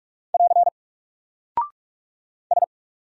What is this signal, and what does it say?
Morse code, heard as a keyed tone, about 45 wpm, 700 Hz. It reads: <AR> S
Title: Single Character (Letter:Number:Punctuation:Prosign) Alt 45wpm